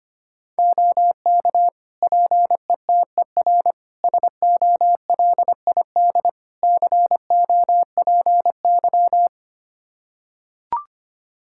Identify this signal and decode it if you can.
Morse code, heard as a keyed tone, about 25 wpm, 700 Hz. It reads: OK PETER SOLID COPY
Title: K1USN SST QSO Practice 25wpm